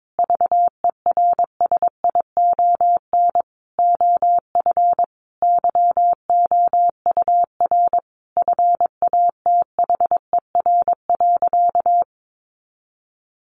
Morse code, heard as a keyed tone, about 22 wpm, 700 Hz. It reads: VERSION OF YOUR FATHER.
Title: Sentences from Top 1400 Words 22wpm